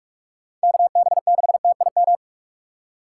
Morse code, heard as a keyed tone, about 44 wpm, 700 Hz. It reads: KB6TIM